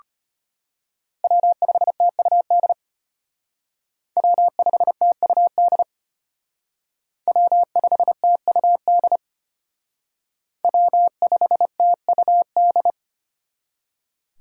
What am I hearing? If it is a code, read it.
Morse code, heard as a keyed tone, about 29 wpm, 700 Hz. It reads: W5TUD W5TUD W5TUD W5TUD
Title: Speed Racing - USA 1x3 Call Signs 25wpm